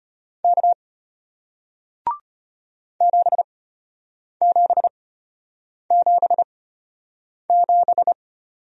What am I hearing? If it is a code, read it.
Morse code, heard as a keyed tone, about 34 wpm, spaced as for 8 wpm, 700 Hz. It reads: K 7777